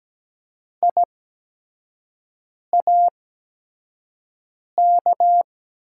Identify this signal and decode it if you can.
Morse code, heard as a keyed tone, about 17 wpm, spaced as for 5 wpm, 700 Hz. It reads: IAK
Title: USA Call Signs - 8x Character Spacing - 17wpm